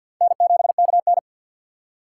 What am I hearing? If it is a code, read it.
Morse code, heard as a keyed tone, about 52 wpm, 700 Hz. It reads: N7KN
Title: USA Call Signs - 4 Characters - 3x Character Spacing - 50wpm